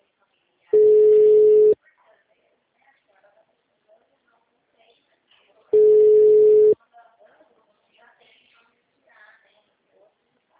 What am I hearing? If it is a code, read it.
Ringback tone (Europe, 425 Hz), with speech.